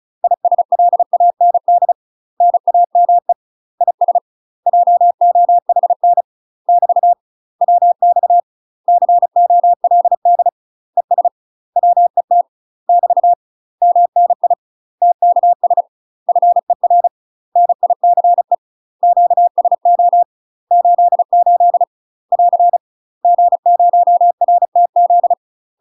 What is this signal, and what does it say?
Morse code, heard as a keyed tone, about 34 wpm, 700 Hz. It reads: ISLAND NAME IS JOHN <BT> WX COLD ES WET <BT> MNI TKS FER NICE QSO 88 <AR> G0RTZ
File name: UK MASTER QSOs TRAINING / 35WPM / UK MASTER QSOs 0026